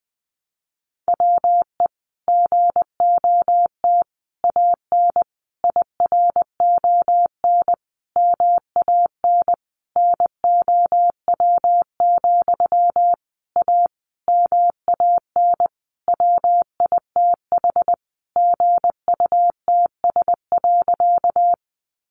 Morse code, heard as a keyed tone, about 20 wpm, 700 Hz. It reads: WE GOT AN IRON MAN NOW, A MAN WITH GUTS.
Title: A Sound Of Thunder 20wpm